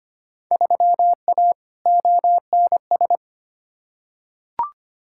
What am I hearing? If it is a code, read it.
Morse code, heard as a keyed tone, about 25 wpm, 700 Hz. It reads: 3A ONS
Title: Field Day Exchanges - 25wpm